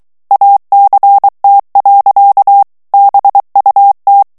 Morse code, heard as a keyed tone, about 23 wpm, 800 Hz. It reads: ACT. BUT
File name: morse mp3 / ANGLAIS texte clair 1 vitesse 22 mots minute